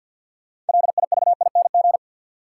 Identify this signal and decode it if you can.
Morse code, heard as a keyed tone, about 53 wpm, 700 Hz. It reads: LIVING
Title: Sentences from Top 1100 Words - New Word Drill Alt - 50wpm